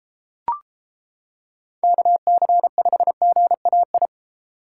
Morse code, heard as a keyed tone, about 31 wpm, 700 Hz. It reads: KC5GAI